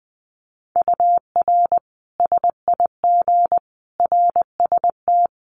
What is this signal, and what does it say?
Morse code, heard as a keyed tone, about 20 wpm, 700 Hz. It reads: UR SIG RST
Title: Common QSO Phrases 20wpm